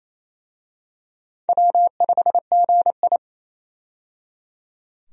Morse code, heard as a keyed tone, about 28 wpm, 700 Hz. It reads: W5GI
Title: USA Call Signs - 4 Characters - 28wpm